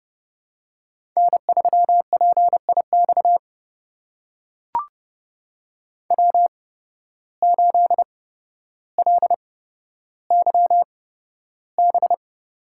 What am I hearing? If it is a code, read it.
Morse code, heard as a keyed tone, about 30 wpm, spaced as for 13 wpm, 700 Hz. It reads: N3PIX W8LYB
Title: USA Call Signs - 5 Characters - 8x Character Spacing - 30wpm